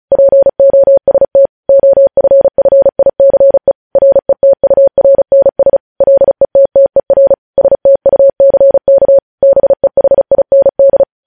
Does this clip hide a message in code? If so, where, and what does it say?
Yes: Morse code from 0.1 to 11.0 s, heard as a keyed tone, about 36 wpm, 550 Hz. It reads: POST OFFICE RETURNS LETTER STUCK BEHIND